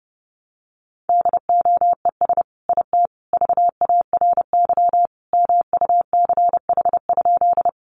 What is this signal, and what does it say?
Morse code, heard as a keyed tone, about 30 wpm, 700 Hz. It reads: DOES IT VARY MUCH?
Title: Sentences from Top 1200 Words 30wpm